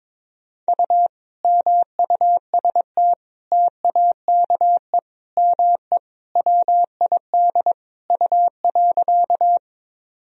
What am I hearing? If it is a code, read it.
Morse code, heard as a keyed tone, about 22 wpm, 700 Hz. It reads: U MUST TAKE ME WID U.
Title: Sentences from Top 200 Words and Abbreviations 22wpm